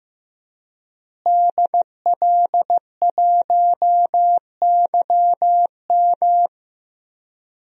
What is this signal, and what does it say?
Morse code, heard as a keyed tone, about 15 wpm, 700 Hz. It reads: DL1YM
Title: DXCC-100 Call Signs - 5 Characters - Repeated 2x - 15wpm